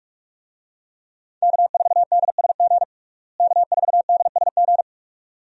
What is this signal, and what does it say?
Morse code, heard as a keyed tone, about 46 wpm, 700 Hz. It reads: K4DSG K4DSG
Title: USA Call Signs - 5 Characters - Repeated 2x - 45wpm